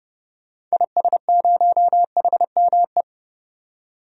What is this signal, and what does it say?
Morse code, heard as a keyed tone, about 29 wpm, 690 Hz. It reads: IS0HME
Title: DXCC-100 Call Signs - 6 Characters - 30wpm